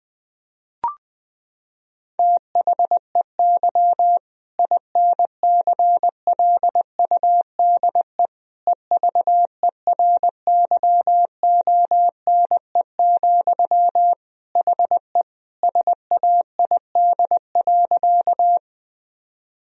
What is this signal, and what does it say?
Morse code, heard as a keyed tone, about 20 wpm, 700 Hz. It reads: THEY INCLUDE EVERYONE, HE SAID.
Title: Sentences from Top 1000 Words - Review Part 3 - 20wpm